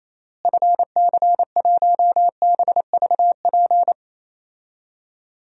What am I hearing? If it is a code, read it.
Morse code, heard as a keyed tone, about 28 wpm, 700 Hz. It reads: FC1BVP
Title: DXCC-100 Call Signs - 6 Characters - Repeated 3x - 28wpm